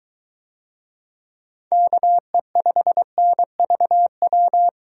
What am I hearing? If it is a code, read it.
Morse code, heard as a keyed tone, about 21 wpm, 700 Hz. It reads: KE5NVW